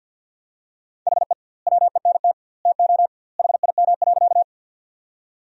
Morse code, heard as a keyed tone, about 49 wpm, 700 Hz. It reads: HE WENT TO HIM.
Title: 2x Word-Spacing - Sentences from Top 200 Words - 50wpm